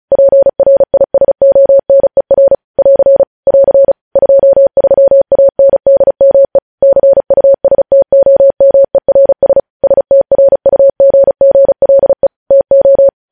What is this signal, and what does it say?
Morse code, heard as a keyed tone, about 36 wpm, 550 Hz. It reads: PRISONER <AR> <AR> 23ANDME CUSTOMERS STRUGGLE TO
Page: News Headlines in Morse Code at 35 WPM